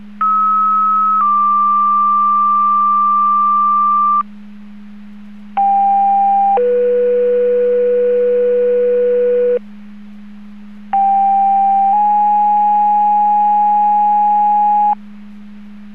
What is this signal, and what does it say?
Two-tone page, heard as a one-second tone, then a three-second tone, about 1290 then 1180 Hz, about 790 then 490 Hz, about 790 then 830 Hz. It is heard from 0.2 to 14.9 s.
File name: Portage 154.385 Paging